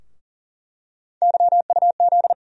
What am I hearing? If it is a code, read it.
Morse code, heard as a keyed tone, about 40 wpm, 700 Hz. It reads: YUZ
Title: USA Call Sign Postfixes - 3 Letters 40wpm